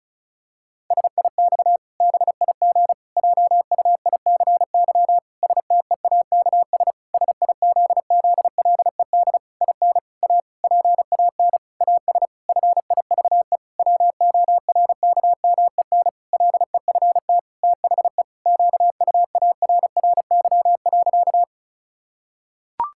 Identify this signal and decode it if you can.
Morse code, heard as a keyed tone, about 34 wpm, 700 Hz. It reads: SIX BIG JUICY STEAKS SIZZLED IN A PAN AS FIVE WORKMEN LEFT THE QUARRY.
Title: Pangram Challenge v2 - 35wpm